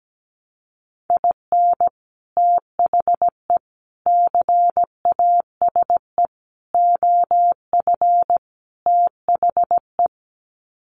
Morse code, heard as a keyed tone, about 17 wpm, 700 Hz. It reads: IN THE CASE OF THE
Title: Sets of 5 Words 17wpm